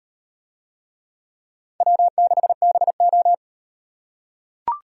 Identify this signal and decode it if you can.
Morse code, heard as a keyed tone, about 38 wpm, 700 Hz. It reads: W6BO